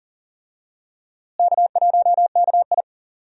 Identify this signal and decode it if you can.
Morse code, heard as a keyed tone, about 38 wpm, 700 Hz. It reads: K1KI
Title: USA Call Signs - 4 Characters - Repeated 2x - 40wpm